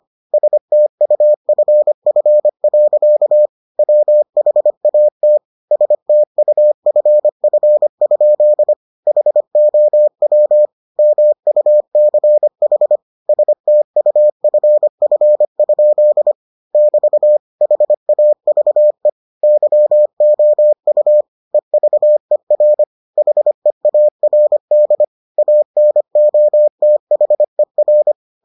Morse code, heard as a keyed tone, about 25 wpm, 600 Hz. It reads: STUFF. WHAT STUFF? HOW MUCH STUFF? <BT> HAVE YOU EVER HEARD ANOTHER